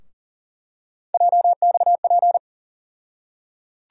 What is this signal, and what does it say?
Morse code, heard as a keyed tone, about 37 wpm, 700 Hz. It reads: JXP
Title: USA Call Sign Postfixes - 3 Letters 40wpm